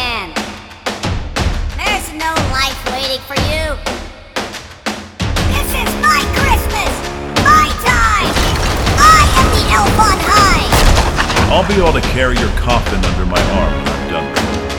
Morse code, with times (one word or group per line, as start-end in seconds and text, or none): none